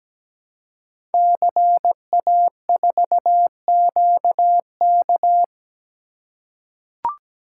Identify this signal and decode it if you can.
Morse code, heard as a keyed tone, about 17 wpm, 700 Hz. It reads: CA4QK
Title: Code Groups - 5 Letter:Numbers - 17wpm